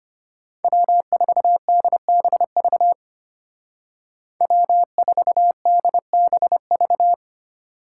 Morse code, heard as a keyed tone, about 31 wpm, 700 Hz. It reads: W4DBV W4DBV
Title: Speed Racing - USA 1x3 Call Signs 25wpm